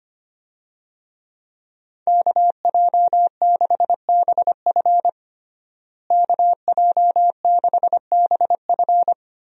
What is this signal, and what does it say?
Morse code, heard as a keyed tone, about 25 wpm, 700 Hz. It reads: KJ6BF KJ6BF
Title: USA Call Signs - 5 Characters - Repeated 2x - 25wpm